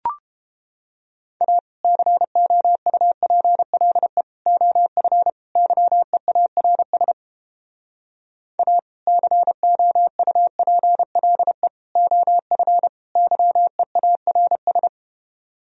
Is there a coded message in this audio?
Yes: Morse code from 1.4 to 14.9 s, heard as a keyed tone, about 30 wpm, 700 Hz. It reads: A COUPLE OF YEARS A COUPLE OF YEARS